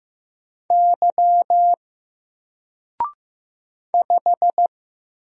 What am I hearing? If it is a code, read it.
Morse code, heard as a keyed tone, about 15 wpm, 700 Hz. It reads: Y 5
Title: Single Character (Letter:Number:Punctuation:Prosign) 15wpm